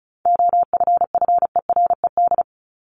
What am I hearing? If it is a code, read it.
Morse code, heard as a keyed tone, about 34 wpm, 700 Hz. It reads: OFFERED
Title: ED Suffix - Top 100 Words with Base Words 35wpm